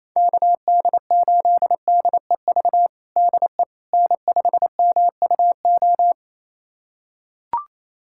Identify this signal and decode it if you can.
Morse code, heard as a keyed tone, about 28 wpm, 700 Hz. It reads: KD8DEV DE N5MUO
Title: USA Callsign DE Callsign 28wpm